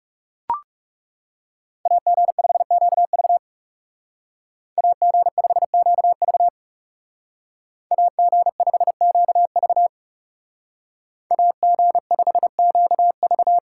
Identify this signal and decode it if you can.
Morse code, heard as a keyed tone, about 39 wpm, 700 Hz. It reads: AG5QV AG5QV AG5QV AG5QV